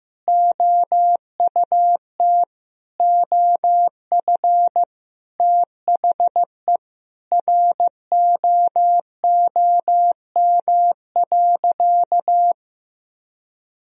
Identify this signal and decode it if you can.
Morse code, heard as a keyed tone, about 15 wpm, 700 Hz. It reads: OUT OF THE ROOM.
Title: Sentences from Top 1000 Words - Review Part 3 - 15wpm